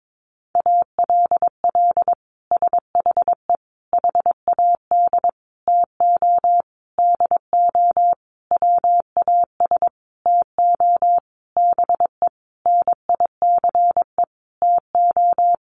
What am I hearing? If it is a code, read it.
Morse code, heard as a keyed tone, about 22 wpm, 700 Hz. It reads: ALL SHE HAD TO DO WAS TO BE NICE TO
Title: Sentences from Top 1100 Words 22wpm